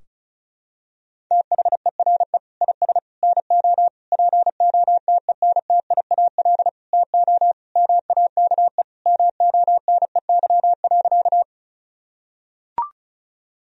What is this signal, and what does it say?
Morse code, heard as a keyed tone, about 34 wpm, 700 Hz. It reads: THERE IS NO POTENTIAL TO MAKE MONEY.
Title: Sentences from Top 1000 Words - Review Part 4 - 35wpm